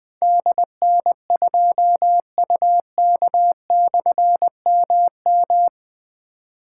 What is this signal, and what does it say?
Morse code, heard as a keyed tone, about 20 wpm, 700 Hz. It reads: DN2UK/MM
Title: Difficult Call Signs 20wpm